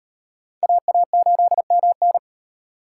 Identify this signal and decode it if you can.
Morse code, heard as a keyed tone, about 38 wpm, 700 Hz. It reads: AA8MN